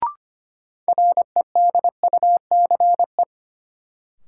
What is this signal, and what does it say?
Morse code, heard as a keyed tone, about 25 wpm, 700 Hz. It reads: REDUCE
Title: Sentences from Top 1100 Words - New Word Drill - 25wpm